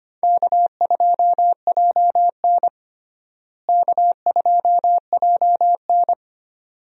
Morse code, heard as a keyed tone, about 25 wpm, 700 Hz. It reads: K2JN K2JN
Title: USA Call Signs - 4 Characters - Repeated 2x - 25wpm